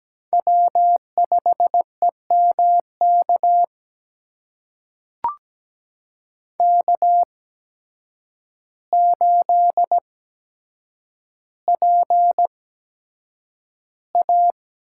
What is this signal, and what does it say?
Morse code, heard as a keyed tone, about 17 wpm, spaced as for 8 wpm, 700 Hz. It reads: W5EMK K8PA
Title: USA Call Signs - 5 Characters - 8x Character Spacing - 17wpm